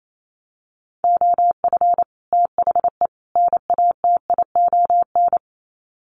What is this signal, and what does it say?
Morse code, heard as a keyed tone, about 28 wpm, 700 Hz. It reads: OF THE NATION
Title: Sets of 3 Words 28wpm